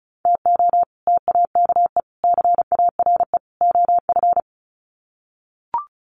Morse code, heard as a keyed tone, about 34 wpm, 700 Hz. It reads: TO TAKE CARE OF
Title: Sets of 4 Words 35wpm